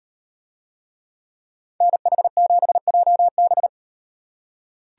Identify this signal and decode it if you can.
Morse code, heard as a keyed tone, about 37 wpm, 700 Hz. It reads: NH7JB